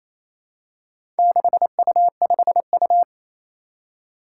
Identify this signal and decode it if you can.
Morse code, heard as a keyed tone, about 27 wpm, 700 Hz. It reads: 6U5U
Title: Four Challenging Letter-Number Combinations 28wpm